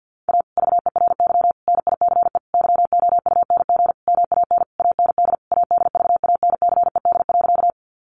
Morse code, heard as a keyed tone, about 50 wpm, 700 Hz. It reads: A VERY NICE YOUNG MAN AND ADVANCED.
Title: Sentences from Top 1200 Words 50wpm